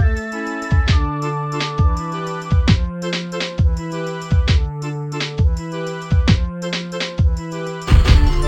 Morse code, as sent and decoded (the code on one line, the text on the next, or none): none